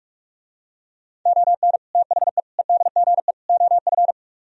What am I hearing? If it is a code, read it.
Morse code, heard as a keyed tone, about 49 wpm, 700 Hz. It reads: ON THE EDGE OF